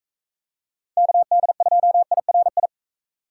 Morse code, heard as a keyed tone, about 41 wpm, 700 Hz. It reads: KD2IRI